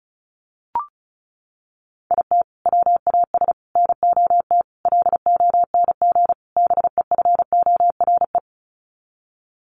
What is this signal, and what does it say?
Morse code, heard as a keyed tone, about 34 wpm, 700 Hz. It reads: IT WAS NOT LONG BEFORE
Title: Sets of 5 Words Encore2 35wpm